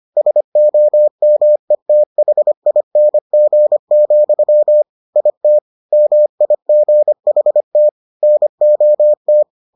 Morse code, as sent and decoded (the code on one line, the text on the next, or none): ... --- -- . - .... .. -. --. --..-- / .. - / -- .. --. .... - / -. --- -
SOMETHING, IT MIGHT NOT